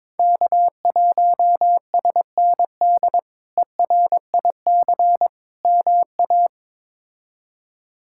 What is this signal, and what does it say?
Morse code, heard as a keyed tone, about 22 wpm, 700 Hz. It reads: K1SND ERIC MA
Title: K1USN SST QSO Practice - 22wpm